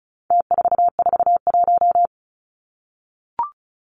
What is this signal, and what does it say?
Morse code, heard as a keyed tone, about 34 wpm, 700 Hz. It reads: T441
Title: Numbers - Contest Style - 4-Digit 35wpm